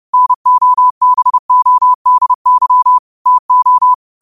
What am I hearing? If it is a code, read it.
Morse code, heard as a keyed tone, about 30 wpm, 1000 Hz. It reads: NOBODY TO
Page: Quotes for Fri, 15 Aug 2025 in Morse Code at 30 words per minute.